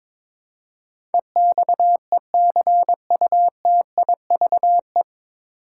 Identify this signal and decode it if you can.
Morse code, heard as a keyed tone, about 21 wpm, 700 Hz. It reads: EXECUTIVE
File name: 9 Letter Words - First 3 Letters Given - 22wpm